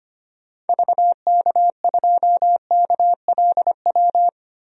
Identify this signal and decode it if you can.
Morse code, heard as a keyed tone, about 25 wpm, 700 Hz. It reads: VK2KLW
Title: DXCC-100 Call Signs - 6 Characters - 25wpm